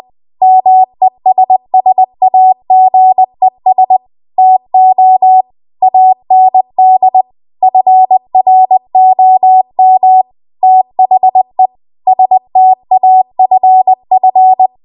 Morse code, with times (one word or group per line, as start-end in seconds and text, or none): 0.4-4.0 MESSAGES
4.4-5.4 TO
5.8-7.2 AND
7.6-10.2 FROM
10.6-11.7 THE
12.1-14.7 STAFF